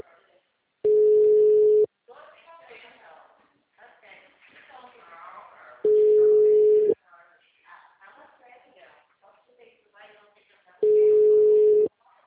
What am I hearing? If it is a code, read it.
Ringback tone (Europe, 425 Hz), with speech.